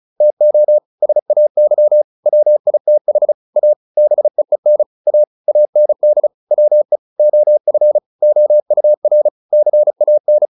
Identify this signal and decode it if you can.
Morse code, heard as a keyed tone, about 34 wpm, 600 Hz. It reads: TO SAY WITH A BEEN A AND WE OF OUR CAN